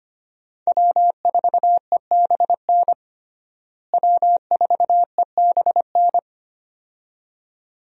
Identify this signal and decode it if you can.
Morse code, heard as a keyed tone, about 25 wpm, 700 Hz. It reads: W4EBN W4EBN
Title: USA Call Signs - 5 Characters - Repeated 2x - 25wpm